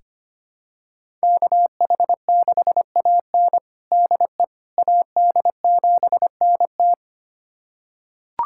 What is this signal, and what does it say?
Morse code, heard as a keyed tone, about 25 wpm, 700 Hz. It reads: KH6AN DE AD7NT
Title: USA Callsign DE Callsign 25wpm